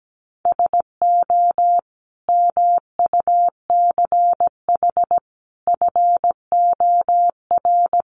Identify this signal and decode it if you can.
Morse code, heard as a keyed tone, about 17 wpm, 700 Hz. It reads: SO MUCH FOR
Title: Sets of 5 Words 17wpm